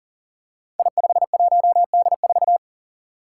Morse code, heard as a keyed tone, about 38 wpm, 690 Hz. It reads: I51D4
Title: Code Groups - 5 Letter:Numbers - 40wpm